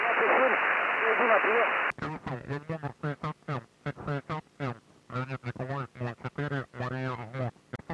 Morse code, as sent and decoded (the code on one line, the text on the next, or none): none